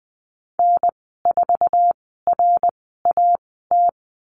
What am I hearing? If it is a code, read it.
Morse code, heard as a keyed tone, about 20 wpm, spaced as for 14 wpm, 700 Hz. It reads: N4RAT
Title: 076 Lesson 13g - Letter L - Call Signs 20wpm